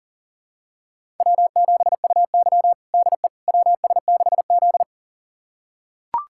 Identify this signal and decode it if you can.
Morse code, heard as a keyed tone, about 37 wpm, 700 Hz. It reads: W7UY DE WS6Z
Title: USA Callsign DE Callsign - 4 Characters - 40wpm